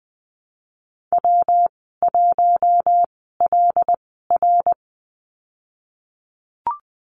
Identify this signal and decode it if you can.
Morse code, heard as a keyed tone, about 20 wpm, spaced as for 15 wpm, 700 Hz. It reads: W1LR
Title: 167 Lesson 27g - Number 7 - Call Signs 20wpm